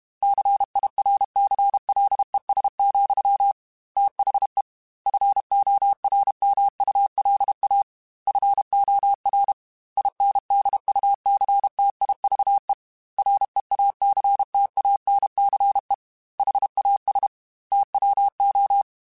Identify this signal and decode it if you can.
Morse code, heard as a keyed tone, about 32 wpm, 800 Hz. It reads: CIRCLES, THE FORMULA FOR INDUCTIVE REACTANCE HAS TWO
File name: Technical Subjects / Technical Subjects-mp3s / XL @30